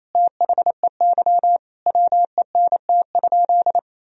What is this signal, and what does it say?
Morse code, heard as a keyed tone, about 27 wpm, 700 Hz. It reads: THEY WENT?
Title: Sentences from Top 1000 Words - Review Part 3 - 28wpm